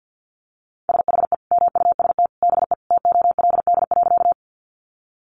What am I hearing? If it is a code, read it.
Morse code, heard as a keyed tone, about 50 wpm, 700 Hz. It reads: SHE MUST BE TOLD.
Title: Sentences from Top 400 Words 50wpm